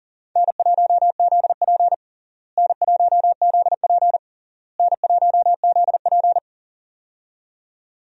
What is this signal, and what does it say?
Morse code, heard as a keyed tone, about 40 wpm, 700 Hz. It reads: N1ZP N1ZP N1ZP
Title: USA Call Signs - 4 Characters - Repeated 3x - 40wpm